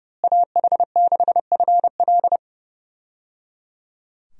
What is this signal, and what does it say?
Morse code, heard as a keyed tone, about 30 wpm, 700 Hz. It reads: AH6FL
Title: USA Call Signs - 5 Characters - Repeated 3x - 30wpm